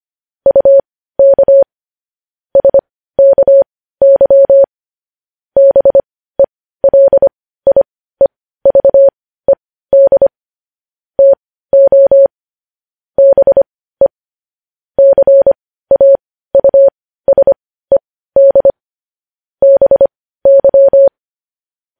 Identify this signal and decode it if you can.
Morse code, heard as a keyed tone, about 25 wpm, spaced as for 15 wpm, 550 Hz. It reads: UK SKY BELIEVED TO BE CAUSED BY
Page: News Headlines in Morse Code at 15 WPM